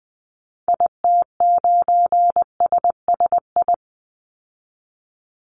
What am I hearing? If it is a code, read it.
Morse code, heard as a keyed tone, about 19 wpm, 700 Hz. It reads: IT9SSI
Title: Call Signs Repeated 3x - 20wpm